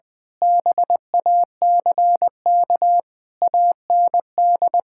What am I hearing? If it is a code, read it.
Morse code, heard as a keyed tone, about 20 wpm, 700 Hz. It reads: BACK AND
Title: Difficult Sentences with Hints - 20wpm